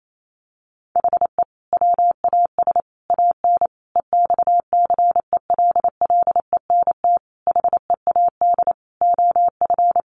Morse code, heard as a keyed tone, about 28 wpm, 700 Hz. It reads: HE WAS AN EXCELLENT HEAD OF
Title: Sentences from Top 1200 Words 28wpm